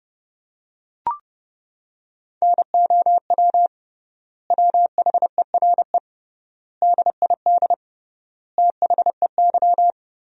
Morse code, heard as a keyed tone, about 32 wpm, 700 Hz. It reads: NOW WHERE DID THEY
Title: 3x Word-Spacing - Sentences from Top 200 Words - 30wpm